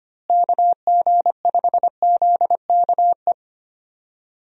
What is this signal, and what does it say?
Morse code, heard as a keyed tone, about 25 wpm, 700 Hz. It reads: KG5ZKE
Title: USA Call Signs - 6 Characters - Repeated 2x - 25wpm